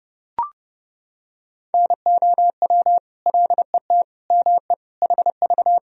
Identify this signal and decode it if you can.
Morse code, heard as a keyed tone, about 32 wpm, 700 Hz. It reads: NOW LET ME HV
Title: Sentences from Top 300 Words and Abbreviations 30wpm